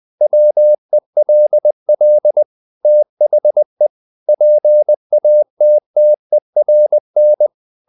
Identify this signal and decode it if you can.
Morse code, heard as a keyed tone, about 20 wpm, 600 Hz. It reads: WELL THE PATTERN
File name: the old man and the sea part 2 20wpm